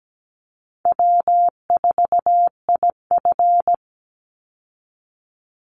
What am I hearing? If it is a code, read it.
Morse code, heard as a keyed tone, about 17 wpm, 700 Hz. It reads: W4IF
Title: Call Signs - 17wpm